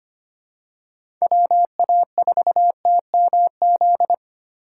Morse code, heard as a keyed tone, about 25 wpm, 700 Hz. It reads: WA4TMZ
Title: USA Call Signs - 6x Character Spacing - 25wpm